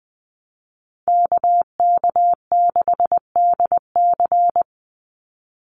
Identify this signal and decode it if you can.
Morse code, heard as a keyed tone, about 20 wpm, 700 Hz. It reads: KK6DC
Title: USA Call Signs 20wpm